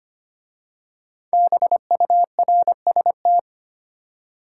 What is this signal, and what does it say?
Morse code, heard as a keyed tone, about 25 wpm, 700 Hz. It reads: BURST
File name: 5 Letter Words - First 3 Letters Given - 25wpm